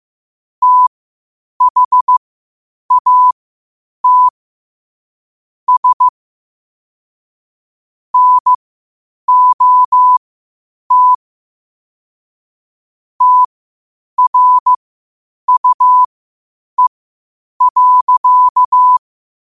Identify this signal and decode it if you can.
Morse code, heard as a keyed tone, about 15 wpm, spaced as for 8 wpm, 1000 Hz. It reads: THAT S NOT TRUE.
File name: qotdfiles / 8wpm